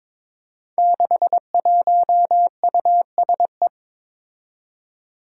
Morse code, heard as a keyed tone, about 22 wpm, 700 Hz. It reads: 61USE
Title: Five Challenging Letter-Number Combinations 22wpm